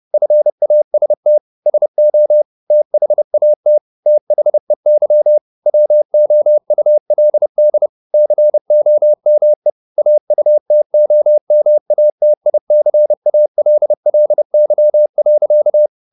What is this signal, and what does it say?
Morse code, heard as a keyed tone, about 30 wpm, 600 Hz. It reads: FAST SO THAT THEY WOULD COME AUTOMATICALLY.
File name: the old man and the sea / the old man and the sea 1 30wpm